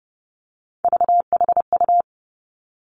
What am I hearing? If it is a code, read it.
Morse code, heard as a keyed tone, about 30 wpm, 700 Hz. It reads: VHU
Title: Three Challenging Letter-Number Combinations 30wpm